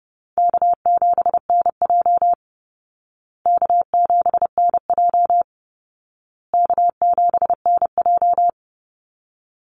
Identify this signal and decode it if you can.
Morse code, heard as a keyed tone, about 30 wpm, 700 Hz. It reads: K7NJ K7NJ K7NJ
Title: Call Signs Repeated 2x - 30wpm